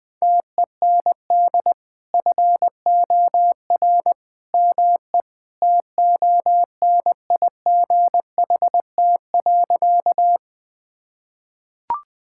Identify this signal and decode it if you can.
Morse code, heard as a keyed tone, about 20 wpm, 700 Hz. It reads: TEND FOR ME TONIGHT.
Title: Sentences from Top 1000 Words - Review Part 5 - 20wpm